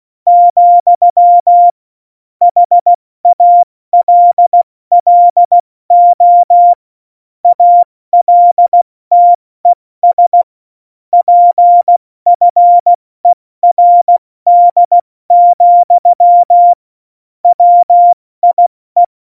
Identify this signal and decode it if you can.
Morse code, heard as a keyed tone, about 16 wpm, spaced as for 14 wpm, 700 Hz. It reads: , HALLO ALTES PFERD, WIE
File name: Morsekurs / Witze / Witz 06. Oktober 2020, 70 F80